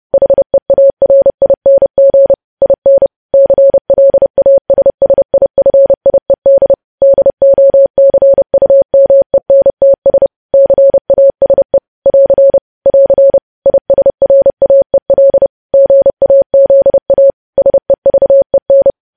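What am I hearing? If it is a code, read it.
Morse code, heard as a keyed tone, about 30 wpm, 550 Hz. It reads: HEARING IN CLASSIFIED DOCUMENTS CASE <AR> <AR> ISRAEL GAZA SEVEN
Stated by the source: Morse code news headlines sent at 30 words per minute.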